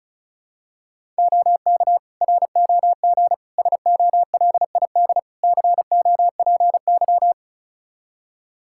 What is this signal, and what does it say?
Morse code, heard as a keyed tone, about 34 wpm, 700 Hz. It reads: OK ROG SOLID COPY
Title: Practice QSO - Exchange 3 - More Info - 35wpm